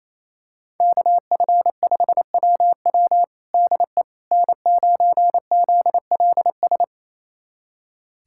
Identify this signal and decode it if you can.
Morse code, heard as a keyed tone, about 28 wpm, 700 Hz. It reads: KF5WW DE N9ZLS
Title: USA Callsign DE Callsign - 5 Characters - 28wpm